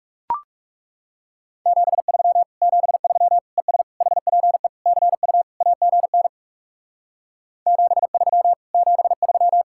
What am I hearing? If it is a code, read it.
Morse code, heard as a keyed tone, about 40 wpm, 700 Hz. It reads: 73 73 ES HPE CU AGN 73 73